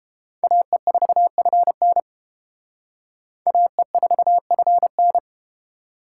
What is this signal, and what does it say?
Morse code, heard as a keyed tone, about 32 wpm, 700 Hz. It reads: AE4FN AE4FN